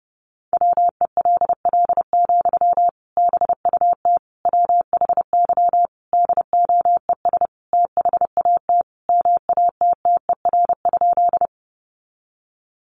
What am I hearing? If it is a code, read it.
Morse code, heard as a keyed tone, about 30 wpm, 700 Hz. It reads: WELL, BUT WHY DOES THAT MATTER?
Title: Sentences from Top 700 Words 30wpm